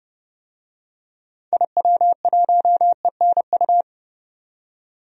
Morse code, heard as a keyed tone, about 30 wpm, 700 Hz. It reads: IW1ENU
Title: DXCC-100 Call Signs - 6 Characters - 30wpm